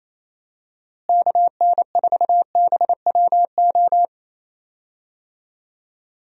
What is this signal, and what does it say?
Morse code, heard as a keyed tone, about 28 wpm, 700 Hz. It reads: KN4BWO
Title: USA Call Signs - 6 Characters - Repeated 2x - 28wpm